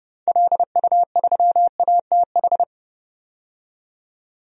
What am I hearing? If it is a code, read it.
Morse code, heard as a keyed tone, about 31 wpm, 700 Hz. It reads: LU3ATH
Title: DXCC-100 Call Signs - 6 Characters - 30wpm